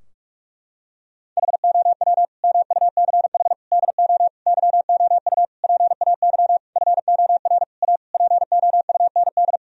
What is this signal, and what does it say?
Morse code, heard as a keyed tone, about 45 wpm, 700 Hz. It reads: HOW MUCH DO YOU PAY FOR A POUND
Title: Sentences from Top 1000 Words - Review Part 3 - 45wpm